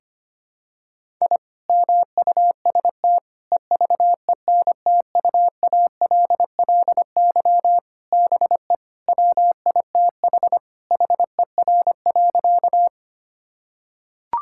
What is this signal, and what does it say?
Morse code, heard as a keyed tone, about 25 wpm, 700 Hz. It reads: I MUST EVENTUALLY BE WITH HER.
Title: Sentences from Top 1000 Words - Review Part 6 - 25wpm